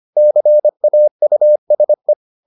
Morse code, heard as a keyed tone, about 25 wpm, 600 Hz. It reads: CAUSE